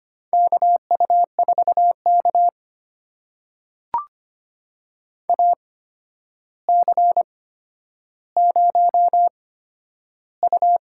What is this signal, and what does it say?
Morse code, heard as a keyed tone, about 25 wpm, spaced as for 11 wpm, 700 Hz. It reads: KU4K AC0U
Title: USA Call Signs - 4 Characters - 8x Character Spacing - 25wpm